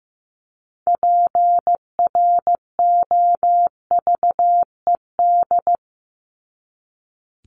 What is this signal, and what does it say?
Morse code, heard as a keyed tone, about 15 wpm, 700 Hz. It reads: PROVED
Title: ED Suffix - Top 100 Words with Base Words 15wpm